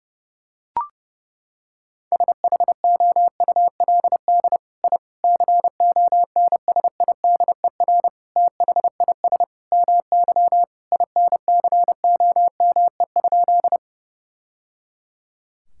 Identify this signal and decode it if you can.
Morse code, heard as a keyed tone, about 30 wpm, 700 Hz. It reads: SHOULD I CONSIDER THIS MY INCOME?
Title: Sentences from Top 1000 Words - Review Part 5 - 30wpm